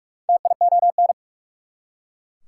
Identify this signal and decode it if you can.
Morse code, heard as a keyed tone, about 43 wpm, 700 Hz. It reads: TION
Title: Prefixes and Suffixes 45wpm